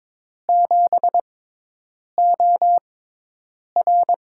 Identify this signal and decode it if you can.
Morse code, heard as a keyed tone, about 22 wpm, spaced as for 8 wpm, 700 Hz. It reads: 7OR
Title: USA Call Signs - 5 Characters - 6x Character Spacing - 22wpm